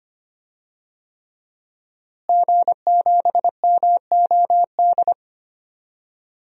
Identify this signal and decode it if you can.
Morse code, heard as a keyed tone, about 25 wpm, 700 Hz. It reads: G7MOD
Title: DXCC-100 Call Signs - 5 Characters - Repeated 3x - 25wpm